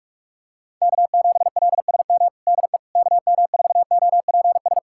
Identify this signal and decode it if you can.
Morse code, heard as a keyed tone, about 44 wpm, 700 Hz. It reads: K7LSM DE KM4OPS
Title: USA Callsign DE Callsign 45wpm